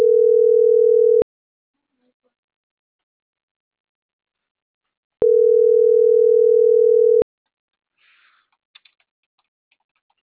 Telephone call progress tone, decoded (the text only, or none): ringback tone